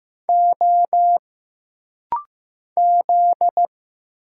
Morse code, heard as a keyed tone, about 15 wpm, 700 Hz. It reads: O Z
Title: Single Letters - Rapid-Fire 15wpm